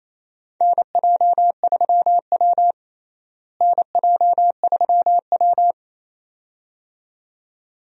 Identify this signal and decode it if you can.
Morse code, heard as a keyed tone, about 28 wpm, 700 Hz. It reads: NJ3W NJ3W
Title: USA Call Signs - 4 Characters - Repeated 2x - 28wpm